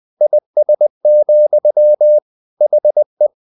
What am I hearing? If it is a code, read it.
Morse code, heard as a keyed tone, about 20 wpm, 600 Hz. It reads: IS, HE
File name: the old man and the sea / the old man and the sea part 1 20wpm